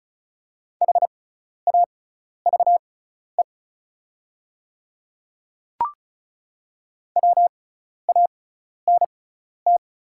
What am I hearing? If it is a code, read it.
Morse code, heard as a keyed tone, about 36 wpm, spaced as for 12 wpm, 700 Hz. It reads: HAVE WANT
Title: 6x Character Spacing - Top 300 Words - 35wpm